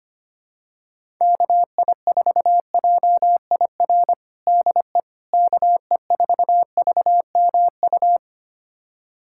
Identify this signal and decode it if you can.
Morse code, heard as a keyed tone, about 25 wpm, 700 Hz. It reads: KI4JIR DE KE4VMU
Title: USA Callsign DE Callsign - 6 Characters - 25wpm